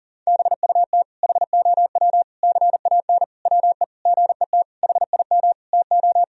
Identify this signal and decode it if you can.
Morse code, heard as a keyed tone, about 40 wpm, 700 Hz. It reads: BUT HOW CAN WE GET HIM TO
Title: Sentences from Top 1000 Words - Review Part 4 - 40wpm